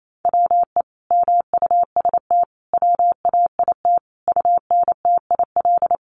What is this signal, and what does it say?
Morse code, heard as a keyed tone, about 28 wpm, 700 Hz. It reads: WE MUST WAIT UNTIL
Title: Sentences from Top 1300 Words 28wpm